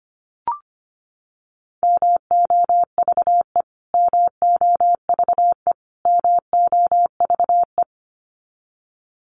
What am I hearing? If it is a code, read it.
Morse code, heard as a keyed tone, about 25 wpm, 700 Hz. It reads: MOVE MOVE MOVE
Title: Top 500 Words Repeated 25wpm